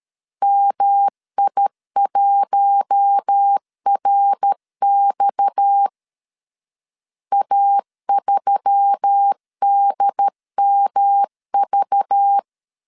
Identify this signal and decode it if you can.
Morse code, heard as a keyed tone, about 13 wpm, 800 Hz. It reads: MI1RX A3DMV
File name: CW / DET / 1 6 1500C S A 12wpm 3440